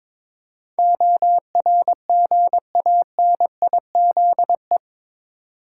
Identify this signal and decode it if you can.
Morse code, heard as a keyed tone, about 22 wpm, 700 Hz. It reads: ORGANIZE
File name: practice / 22wpm / Eight Letter Words - 22wpm